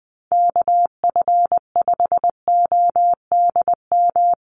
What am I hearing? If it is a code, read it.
Morse code, heard as a keyed tone, about 20 wpm, 700 Hz. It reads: KF5ODM
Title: USA Call Signs Repeated 3x 20wpm